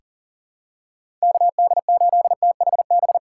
Morse code, heard as a keyed tone, about 40 wpm, 700 Hz. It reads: KD8THB
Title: USA Call Signs - 6 Characters - 6x Character Spacing - 40wpm